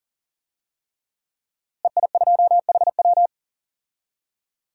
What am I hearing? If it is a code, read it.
Morse code, heard as a keyed tone, about 40 wpm, 700 Hz. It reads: EI2HW
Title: DXCC-100 Call Signs - 5 Characters - Repeated 3x - 40wpm